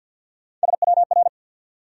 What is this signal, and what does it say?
Morse code, heard as a keyed tone, about 53 wpm, 700 Hz. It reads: SWR
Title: Top 180 QSO Elements 50wpm